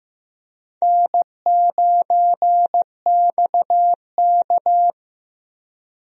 Morse code, heard as a keyed tone, about 16 wpm, 700 Hz. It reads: N9XK
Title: USA Call Signs - 4 Characters - Repeated 2x - 15wpm